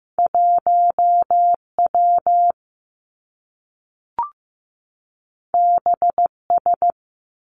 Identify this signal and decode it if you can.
Morse code, heard as a keyed tone, about 15 wpm, 700 Hz. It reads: 1 W B S
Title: Two Challenging Letter-Number Combinations 15wpm